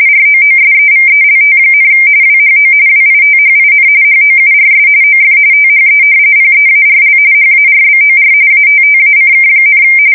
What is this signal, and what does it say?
RTTY, about 45.45 baud. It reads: ROOSEVELT, WHO WITHIN HOURS OF THE PEARL HARBOR ATTACK,
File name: fsk170 / usnavy4